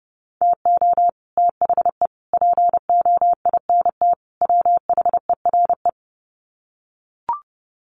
Morse code, heard as a keyed tone, about 29 wpm, 700 Hz. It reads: TO THE POINT WHERE
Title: Sets of 4 Words 30wpm